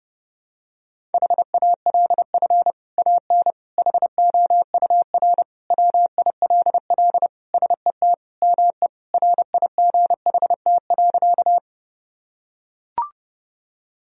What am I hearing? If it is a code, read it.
Morse code, heard as a keyed tone, about 30 wpm, 700 Hz. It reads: HALF AN HOUR WILL SET ME RIGHT.
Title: Sentences from Top 1000 Words - Review Part 1 - 30wpm